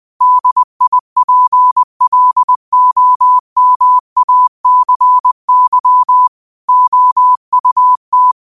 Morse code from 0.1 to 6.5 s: DIPLOMACY

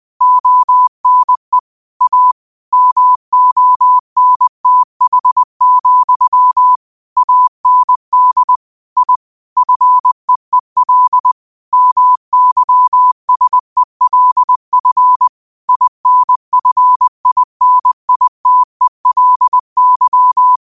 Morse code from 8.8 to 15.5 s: I FEEL MYSELF

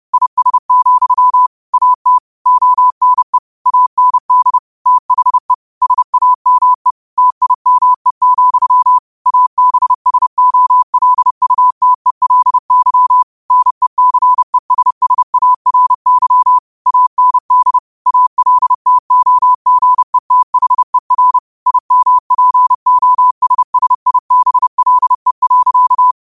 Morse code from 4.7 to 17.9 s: THE SAME TIME, ABSOLUTELY NECESSARY AND

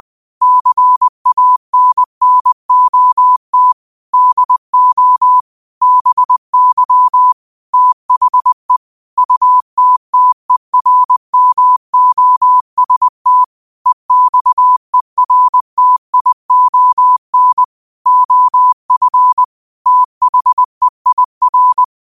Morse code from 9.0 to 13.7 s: UTTERMOST

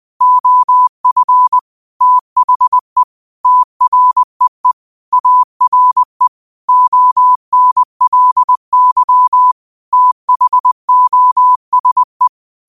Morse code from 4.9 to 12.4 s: ARE ONLY THOSE

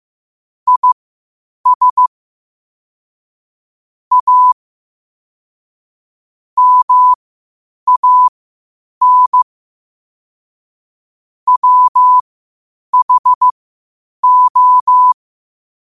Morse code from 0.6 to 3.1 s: IS

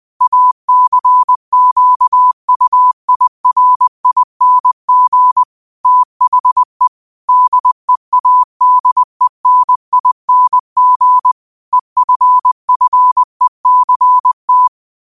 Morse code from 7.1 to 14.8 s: DEADENING EFFECT

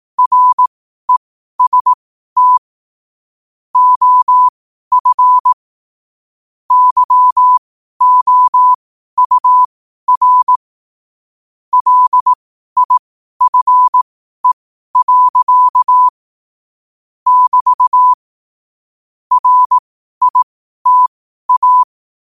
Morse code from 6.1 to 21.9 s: YOUR LIFE. <BT> RITA

